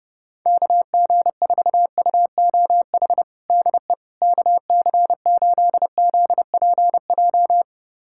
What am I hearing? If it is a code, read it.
Morse code, heard as a keyed tone, about 32 wpm, 700 Hz. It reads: KG4UOH DE KC8ZPJ
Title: USA Callsign DE Callsign - 6 Characters - 30wpm